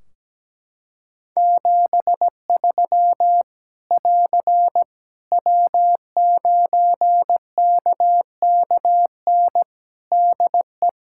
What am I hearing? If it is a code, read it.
Morse code, heard as a keyed tone, about 17 wpm, 700 Hz. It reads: 73 <AR> W9KKN DE
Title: K1USN SST QSO Practice 17wpm